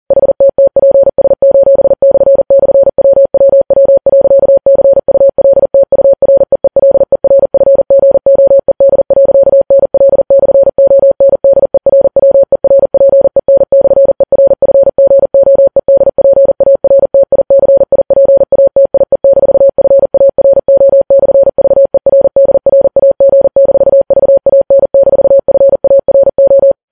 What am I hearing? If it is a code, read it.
Morse code, heard as a keyed tone, about 40 wpm, 550 Hz. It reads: HTTPS://WWW.CULTUREELERFGOED.NL/ONDERWERPEN/ERFGOEDPARTICIPATIE-FARO/VERDRAG-VAN-FARO
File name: Files Web tls / Morse on9fv / morse40kps